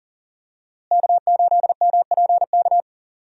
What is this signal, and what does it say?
Morse code, heard as a keyed tone, about 40 wpm, 700 Hz. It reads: K8MPK
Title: USA Call Signs - 5 Characters - Repeated 2x - 40wpm